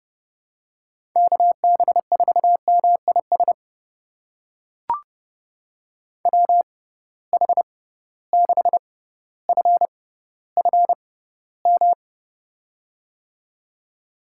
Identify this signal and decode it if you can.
Morse code, heard as a keyed tone, about 30 wpm, spaced as for 15 wpm, 700 Hz. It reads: KB4MIS WH6FFM
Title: USA Call Signs - 6 Characters - 6x Character Spacing - 30wpm